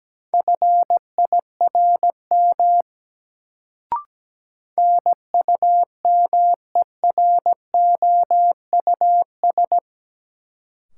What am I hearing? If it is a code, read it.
Morse code, heard as a keyed tone, about 17 wpm, 700 Hz. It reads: FIRM NUMEROUS
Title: Sentences from Top 1400 Words - New Word Drill - 17wpm